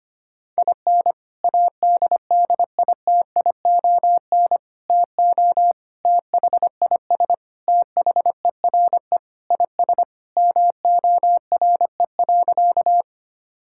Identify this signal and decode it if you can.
Morse code, heard as a keyed tone, about 25 wpm, 700 Hz. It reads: IN ADDITION TO THIS THERE IS MORE.
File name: practice / 25wpm / Sentences with 3-Word Phrases - 25wpm